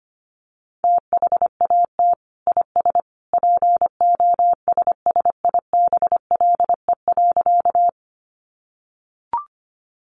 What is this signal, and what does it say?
Morse code, heard as a keyed tone, about 25 wpm, 700 Hz. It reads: THAT IS POSSIBLE.
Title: Sentences from Top 500 Words 25wpm